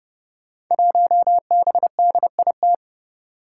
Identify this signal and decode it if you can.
Morse code, heard as a keyed tone, about 30 wpm, 700 Hz. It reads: 1BDIT
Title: Five Challenging Letter-Number Combinations 30wpm